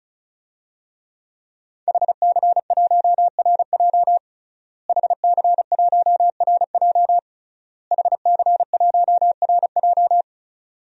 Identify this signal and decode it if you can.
Morse code, heard as a keyed tone, about 36 wpm, 700 Hz. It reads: HC1RJ HC1RJ HC1RJ
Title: DXCC-100 Call Signs - 5 Characters - Repeated 3x - 35wpm